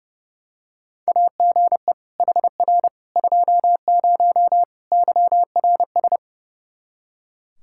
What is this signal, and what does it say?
Morse code, heard as a keyed tone, about 32 wpm, 700 Hz. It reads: AGE HR 20 YRS
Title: Practice QSO - Exchange 3 - More Info - 30wpm